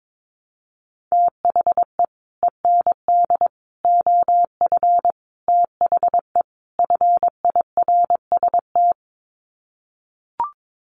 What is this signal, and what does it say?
Morse code, heard as a keyed tone, about 22 wpm, 700 Hz. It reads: THE END OF THE FIRST
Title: Sets of 5 Words Encore 22wpm